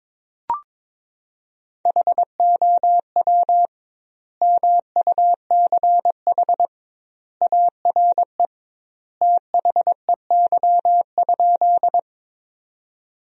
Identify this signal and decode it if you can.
Morse code, heard as a keyed tone, about 22 wpm, 700 Hz. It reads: HOW MUCH ARE THEY?
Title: 2x Word-Spacing - Sentences from Top 200 Words - 22wpm